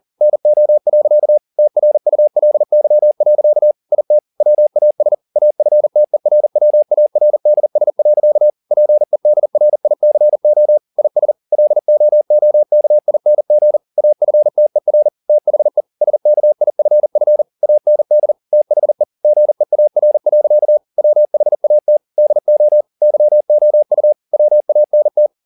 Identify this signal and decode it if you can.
Morse code, heard as a keyed tone, about 40 wpm, 600 Hz. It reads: NO. TRULY. IT WAS AFTERWARDS. PEDRICO IS LOOKING AFTER THE SKIFF AND THE GEAR. WHAT DO YOU WANT
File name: the old man and the sea 2 40wpm